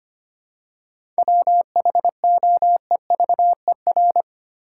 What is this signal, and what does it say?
Morse code, heard as a keyed tone, about 25 wpm, 700 Hz. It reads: WHOEVER
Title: Seven Letter Words - 25wpm